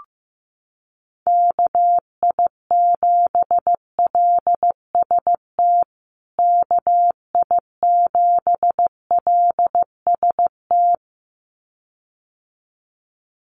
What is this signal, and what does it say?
Morse code, heard as a keyed tone, about 15 wpm, 700 Hz. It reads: KI7LST KI7LST
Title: USA Call Signs Repeated 2x 15wpm